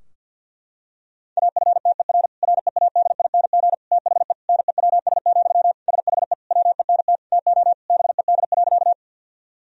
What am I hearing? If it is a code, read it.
Morse code, heard as a keyed tone, about 51 wpm, 700 Hz. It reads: AFTER READING THE NEWS, SHE WENT TO BED.
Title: Sentences from Top 1000 Words - Review Part 3 - 50wpm